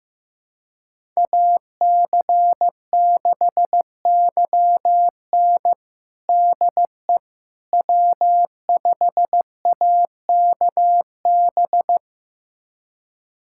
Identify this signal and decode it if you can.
Morse code, heard as a keyed tone, about 15 wpm, 700 Hz. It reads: AC6YN DE W5AKB
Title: USA Callsign DE Callsign - 5 Characters - 15wpm